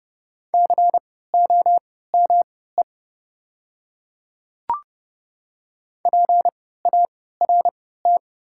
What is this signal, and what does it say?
Morse code, heard as a keyed tone, about 30 wpm, spaced as for 17 wpm, 690 Hz. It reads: COME PART
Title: 3x Character Spacing - Top 200 Words - 30wpm